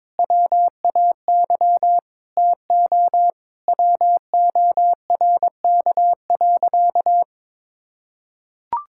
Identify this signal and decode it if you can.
Morse code, heard as a keyed tone, about 22 wpm, 700 Hz. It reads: WAY TO WORK.
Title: Sentences from Top 1000 Words - Review Part 5 - 22wpm